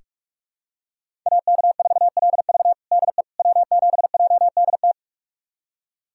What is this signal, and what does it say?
Morse code, heard as a keyed tone, about 46 wpm, 700 Hz. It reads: AK4LV DE W7JDT
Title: USA Callsign DE Callsign - 5 Characters - 45wpm